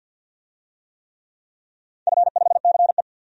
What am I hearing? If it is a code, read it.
Morse code, heard as a keyed tone, about 52 wpm, 700 Hz. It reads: F5CE
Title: DXCC-100 Call Signs - 4 Characters - 50wpm